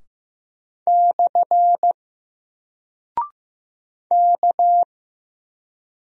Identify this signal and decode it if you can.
Morse code, heard as a keyed tone, about 15 wpm, 700 Hz. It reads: / K
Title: Single Character (Letter:Number:Punctuation:Prosign) 15wpm